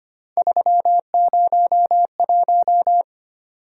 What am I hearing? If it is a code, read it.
Morse code, heard as a keyed tone, about 25 wpm, 700 Hz. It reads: 301
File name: Three Letter-Number Combinations - 25wpm